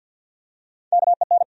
Morse code, heard as a keyed tone, about 54 wpm, 700 Hz. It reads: KEN
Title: USA Common Names - Max 3 Letters - 50wpm